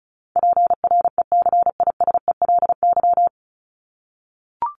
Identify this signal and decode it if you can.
Morse code, heard as a keyed tone, about 34 wpm, 700 Hz. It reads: PRECISELY
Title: LY Suffix - Top 100 Words with Base Words 35wpm